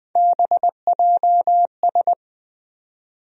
Morse code, heard as a keyed tone, about 20 wpm, 700 Hz. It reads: BJS
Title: USA Call Sign Postfixes - 3 Letters 20wpm